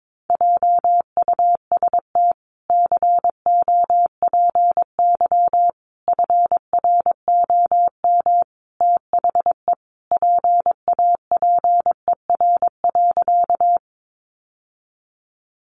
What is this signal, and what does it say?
Morse code, heard as a keyed tone, about 22 wpm, 700 Hz. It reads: JUST COPY FROM THE PAPER.
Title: Sentences from Top 1400 Words 22wpm